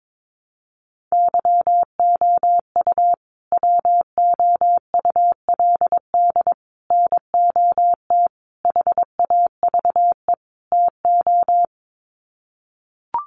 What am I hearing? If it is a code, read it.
Morse code, heard as a keyed tone, about 22 wpm, 700 Hz. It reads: YOU WOULD NOT HAVE TO
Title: Sets of 5 Words Encore 22wpm